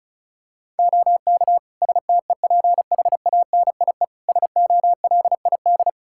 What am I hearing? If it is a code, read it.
Morse code, heard as a keyed tone, about 36 wpm, 700 Hz. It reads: OK STEPHANIE SOLID
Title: Practice QSO - Exchange 3 - More Info - 35wpm